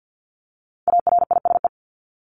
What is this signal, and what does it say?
Morse code, heard as a keyed tone, about 50 wpm, 700 Hz. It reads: ARISE
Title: Top 1501-1750 Words 50wpm